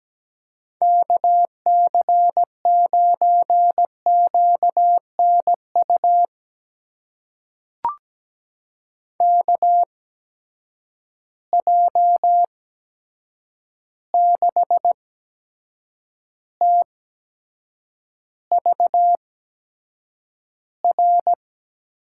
Morse code, heard as a keyed tone, about 17 wpm, spaced as for 7 wpm, 700 Hz. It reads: KC9QNU KJ6TVR
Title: USA Call Signs - 6 Characters - 8x Character Spacing - 17wpm